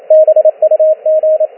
Morse code, heard as a keyed tone, about 28 wpm, 620 Hz, with radio noise. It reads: BUG